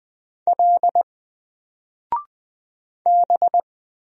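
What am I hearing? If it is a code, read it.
Morse code, heard as a keyed tone, about 20 wpm, 700 Hz. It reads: L B
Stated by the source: Single letter and number Morse drill at 20 wpm, from N4LCD.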